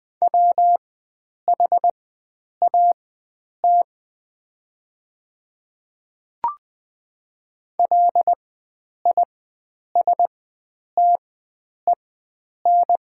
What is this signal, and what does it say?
Morse code, heard as a keyed tone, about 20 wpm, spaced as for 9 wpm, 700 Hz. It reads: WHAT LISTEN
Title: 4x Character Spacing - Top 500 Words - 20wpm